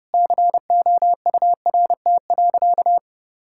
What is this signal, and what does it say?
Morse code, heard as a keyed tone, about 30 wpm, 700 Hz. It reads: COURT.
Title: Sentences from Top 1600 Words 30wpm